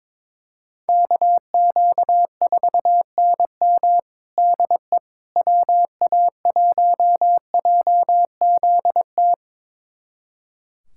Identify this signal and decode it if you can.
Morse code, heard as a keyed tone, about 22 wpm, 700 Hz. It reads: KQ4NM DE WA1JZT
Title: USA Callsign DE Callsign 22wpm